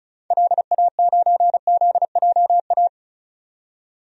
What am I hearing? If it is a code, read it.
Morse code, heard as a keyed tone, about 36 wpm, 700 Hz. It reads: LA9ZJA
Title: DXCC-100 Call Signs - 6 Characters - 35wpm